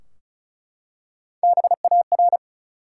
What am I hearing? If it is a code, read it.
Morse code, heard as a keyed tone, about 36 wpm, 700 Hz. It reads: BAR
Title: Three Letter Words - 35wpm